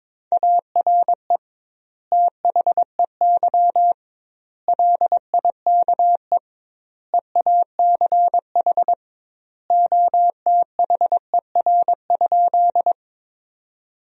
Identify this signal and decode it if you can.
Morse code, heard as a keyed tone, about 22 wpm, 700 Hz. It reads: ARE THEY LIKE EACH OTHER?
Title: 2x Word-Spacing - Sentences from Top 200 Words - 22wpm